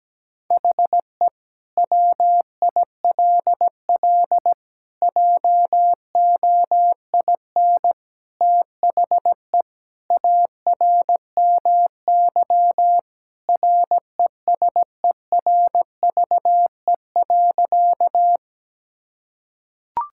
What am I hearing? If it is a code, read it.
Morse code, heard as a keyed tone, about 17 wpm, 700 Hz. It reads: HE WILL JOIN THE ARMY RESERVE.
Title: Sentences from Top 1000 Words - Review Part 6 - 17wpm